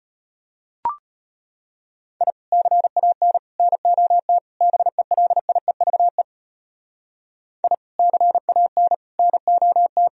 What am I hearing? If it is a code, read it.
Morse code, heard as a keyed tone, about 34 wpm, 700 Hz. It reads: I CAN NOT BELIEVE I CAN NOT